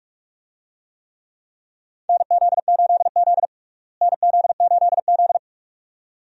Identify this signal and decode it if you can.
Morse code, heard as a keyed tone, about 46 wpm, 700 Hz. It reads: NZ8Z NZ8Z
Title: USA Call Signs - 4 Characters - Repeated 2x - 45wpm